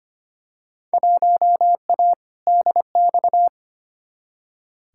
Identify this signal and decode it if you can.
Morse code, heard as a keyed tone, about 25 wpm, 700 Hz. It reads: 1A DX
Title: Field Day Exchanges Encore - 25wpm